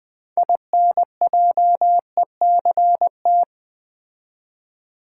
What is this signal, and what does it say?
Morse code, heard as a keyed tone, about 20 wpm, 700 Hz. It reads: INJECT
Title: Build A Word - 3 Letters At A Time - 9-Letter Words - 20wpm